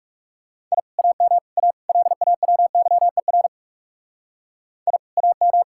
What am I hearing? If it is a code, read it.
Morse code, heard as a keyed tone, about 38 wpm, 700 Hz. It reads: I AM A LAWYER I AM